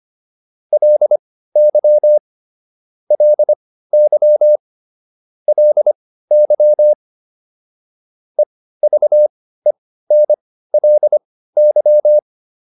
Morse code, heard as a keyed tone, about 25 wpm, spaced as for 15 wpm, 600 Hz. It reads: LY LY LY EVENLY